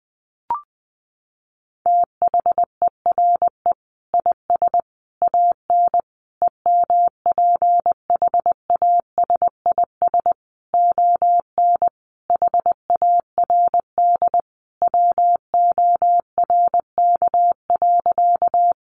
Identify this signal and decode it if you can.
Morse code, heard as a keyed tone, about 20 wpm, 700 Hz. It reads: THERE IS AN EMPHASIS ON HARD WORK.
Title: Sentences from Top 1400 Words 20wpm